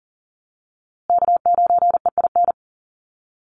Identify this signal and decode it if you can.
Morse code, heard as a keyed tone, about 40 wpm, 700 Hz. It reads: K9EIN
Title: USA Call Signs Repeated 2x 40wpm